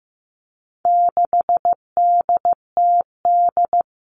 Morse code, heard as a keyed tone, about 15 wpm, 700 Hz. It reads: 6DTD
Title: All Cut Numbers - 4-Digit 15wpm